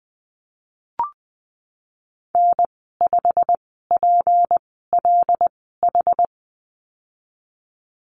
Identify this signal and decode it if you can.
Morse code, heard as a keyed tone, about 20 wpm, spaced as for 15 wpm, 700 Hz. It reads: N5PLH
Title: 161 Lesson 26g - Letter g - Call Signs 20wpm